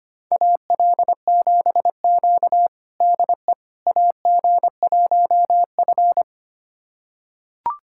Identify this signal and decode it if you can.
Morse code, heard as a keyed tone, about 25 wpm, 700 Hz. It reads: AL7Q DE AG1F
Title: USA Callsign DE Callsign - 4 Characters - 25wpm